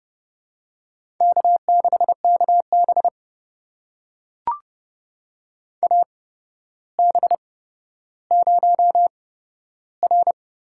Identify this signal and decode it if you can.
Morse code, heard as a keyed tone, about 30 wpm, spaced as for 13 wpm, 700 Hz. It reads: K6KB AB0R
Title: USA Call Signs - 8x Character Spacing - 30wpm